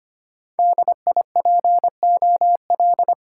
Morse code, heard as a keyed tone, about 25 wpm, 700 Hz. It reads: D I P O L
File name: Top 90 QSO Elements - 6x Character Spacing - 25wpm